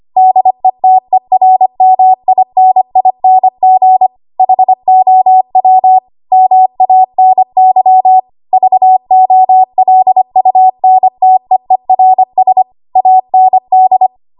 Morse code, heard as a keyed tone, about 25 wpm, 750 Hz. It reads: DETERMINING HOW MANY VOLUNTEERS AND
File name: Morse / 25WPM